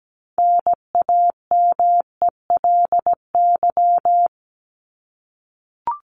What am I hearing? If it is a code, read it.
Morse code, heard as a keyed tone, about 17 wpm, 700 Hz. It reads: NAMELY
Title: LY Suffix - Top 100 Words with Base Words 17wpm